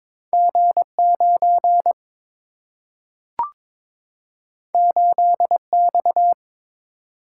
Morse code, heard as a keyed tone, about 22 wpm, 700 Hz. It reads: G 9 8 X
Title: Single-Letter Single-Number 22wpm